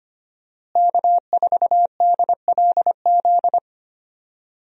Morse code, heard as a keyed tone, about 25 wpm, 700 Hz. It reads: K4DLZ
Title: USA Call Signs - 5 Characters - Repeated 3x - 25wpm